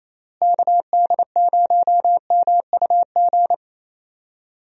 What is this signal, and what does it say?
Morse code, heard as a keyed tone, about 28 wpm, 700 Hz. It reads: KD0MUG
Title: USA Call Signs - 6 Characters - Repeated 3x - 28wpm